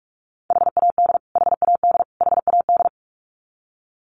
Morse code, heard as a keyed tone, about 44 wpm, 700 Hz. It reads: HAD HAD HAD
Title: Top 500 Words Repeated 45wpm